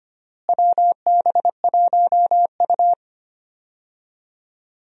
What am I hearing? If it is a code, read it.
Morse code, heard as a keyed tone, about 25 wpm, 700 Hz. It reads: WB1U
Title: USA Call Signs - 4 Characters - Repeated 3x - 25wpm